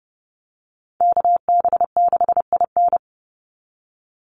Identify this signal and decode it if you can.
Morse code, heard as a keyed tone, about 30 wpm, 700 Hz. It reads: KB6IN
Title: USA Call Signs 30wpm